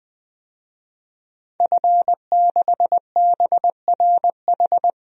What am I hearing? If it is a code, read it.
Morse code, heard as a keyed tone, about 20 wpm, 700 Hz. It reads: F6BRH
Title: DXCC-100 Call Signs - 5 Characters - 20wpm